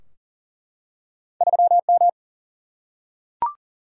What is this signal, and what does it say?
Morse code, heard as a keyed tone, about 40 wpm, 700 Hz. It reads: 3M
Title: Single-Letter Single-Number 40wpm